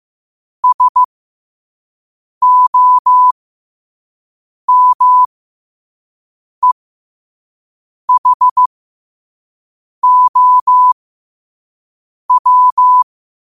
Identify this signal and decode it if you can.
Morse code, heard as a keyed tone, about 15 wpm, spaced as for 5 wpm, 1000 Hz. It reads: SOMEHOW
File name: qotdfiles / 5wpm